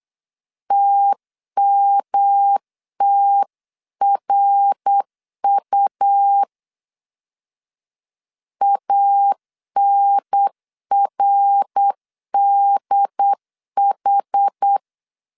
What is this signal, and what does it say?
Morse code, heard as a keyed tone, about 8 wpm, 800 Hz. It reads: TMTRU ANRDH
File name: CW / DET / 1 3 1500C S C 8wpm 4214